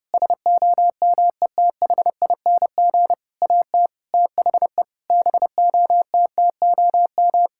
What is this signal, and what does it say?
Morse code, heard as a keyed tone, about 30 wpm, 700 Hz. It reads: SOMETHING AT THE BOTTOM
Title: Sentences from Top 1000 Words - Review Part 6 - 30wpm